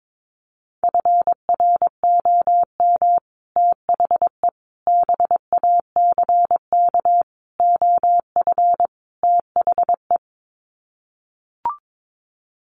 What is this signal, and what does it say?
Morse code, heard as a keyed tone, about 22 wpm, 700 Hz. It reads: FROM THE BACK OF THE
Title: Sets of 5 Words Encore 22wpm